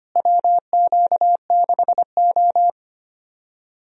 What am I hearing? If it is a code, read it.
Morse code, heard as a keyed tone, about 25 wpm, 700 Hz. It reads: WQ6O
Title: USA Call Signs - 4 Characters - Repeated 3x - 25wpm